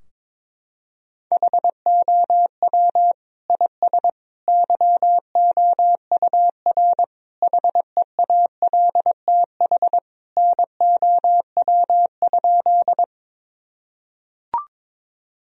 Morse code, heard as a keyed tone, about 22 wpm, 700 Hz. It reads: HOW IS YOUR HEALTH NOW?
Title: Sentences from Top 1000 Words - Review Part 3 - 22wpm